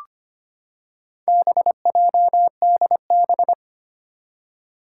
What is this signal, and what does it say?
Morse code, heard as a keyed tone, about 25 wpm, 700 Hz. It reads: BJDB
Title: Four Challenging Letter-Number Combinations 25wpm